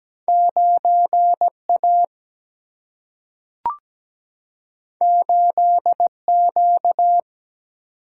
Morse code, heard as a keyed tone, about 17 wpm, 700 Hz. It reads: 9 A 8 Q
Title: Single-Letter Single-Number 17wpm